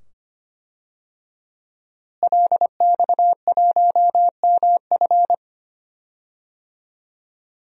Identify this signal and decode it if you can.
Morse code, heard as a keyed tone, about 25 wpm, 700 Hz. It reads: LX1MF
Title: DXCC-100 Call Signs - 5 Characters - Repeated 2x - 25wpm